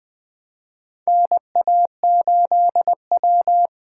Morse code, heard as a keyed tone, about 21 wpm, 700 Hz. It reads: NA8W
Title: USA Callsign DE Callsign - 4 Characters - 20wpm